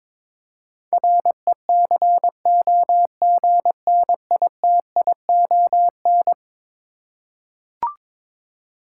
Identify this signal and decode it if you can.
Morse code, heard as a keyed tone, about 21 wpm, 700 Hz. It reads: RECOGNITION
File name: practice / 22wpm / Eleven Letter Words - 22wpm